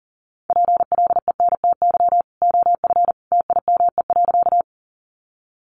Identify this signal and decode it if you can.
Morse code, heard as a keyed tone, about 40 wpm, 700 Hz. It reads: PLENTY OF TIME.
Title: Sentences from Top 1300 Words 40wpm